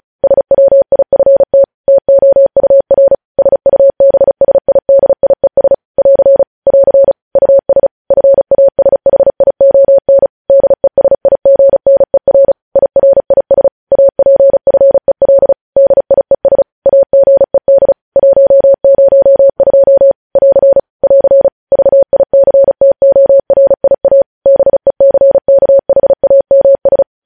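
Morse code, heard as a keyed tone, about 36 wpm, 550 Hz. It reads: SWIFT TOUR SUBSIDIES <AR> <AR> US FASHION DESIGNER IRIS APFEL DIES AGED 102 <AR> <AR> VICTORIA BECKHAMS
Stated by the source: Morse code news headlines sent at 35.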